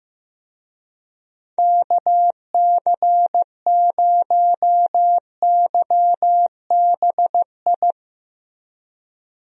Morse code, heard as a keyed tone, about 15 wpm, 700 Hz. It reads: KC0YBI
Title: USA Call Signs - 6 Characters - Repeated 2x - 15wpm